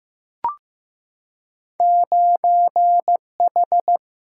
Morse code, heard as a keyed tone, about 15 wpm, 700 Hz. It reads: 9H
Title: Single-Letter Single-Number 15wpm